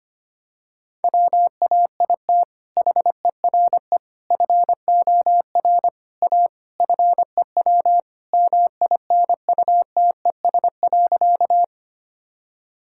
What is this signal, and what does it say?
Morse code, heard as a keyed tone, about 25 wpm, 700 Hz. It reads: WAIT HERE FOR A FEW MINUTES.
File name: practice / 25wpm / Sentences with 3-Word Phrases - 25wpm